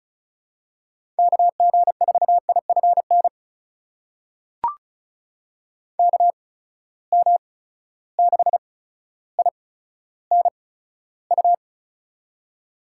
Morse code, heard as a keyed tone, about 36 wpm, spaced as for 15 wpm, 700 Hz. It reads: KG4IFN KM6INU
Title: USA Call Signs - 6 Characters - 8x Character Spacing - 35wpm